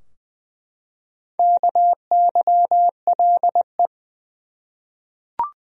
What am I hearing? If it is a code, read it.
Morse code, heard as a keyed tone, about 20 wpm, 700 Hz. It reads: KYLE
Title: USA Common Names - 20wpm